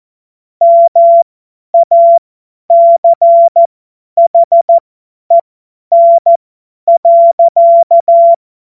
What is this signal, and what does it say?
Morse code, heard as a keyed tone, about 14 wpm, spaced as for 10 wpm, 680 Hz. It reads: MACHEN.
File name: Morsekurs / Witze / Witz 30 Januar 2024 50 F70